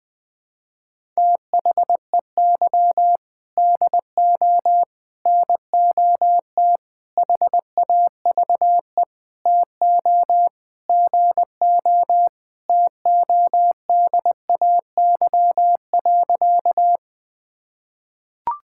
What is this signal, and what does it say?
Morse code, heard as a keyed tone, about 21 wpm, 700 Hz. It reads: THEY DO NOT HAVE TO GO TODAY.
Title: Sentences with 5-Word Phrases - Spoken Beginning - 20wpm